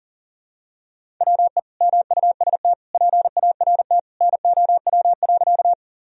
Morse code, heard as a keyed tone, about 37 wpm, 700 Hz. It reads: WE MUST PART NOW.
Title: Sentences from Top 1000 Words - Review Part 1 - 40wpm